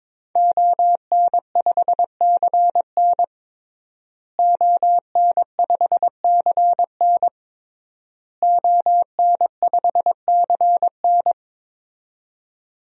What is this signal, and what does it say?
Morse code, heard as a keyed tone, about 22 wpm, 700 Hz. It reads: ON5CN ON5CN ON5CN
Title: DXCC-100 Call Signs - 5 Characters - Repeated 3x - 22wpm